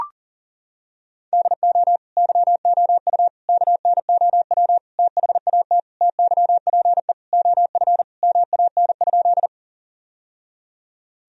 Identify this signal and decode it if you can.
Morse code, heard as a keyed tone, about 40 wpm, 700 Hz. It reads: DO YOU KNOW THAT TYPE OF MAN?
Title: Sentences from Top 1000 Words - Review Part 3 - 40wpm